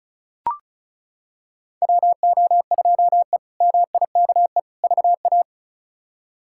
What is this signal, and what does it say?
Morse code, heard as a keyed tone, about 34 wpm, 700 Hz. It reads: WO2E MIKE VA
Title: K1USN SST QSO Practice - 35wpm